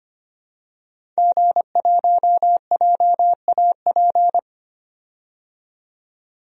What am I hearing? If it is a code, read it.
Morse code, heard as a keyed tone, about 25 wpm, 700 Hz. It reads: G1JAP
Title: DXCC-100 Call Signs - 5 Characters - 25wpm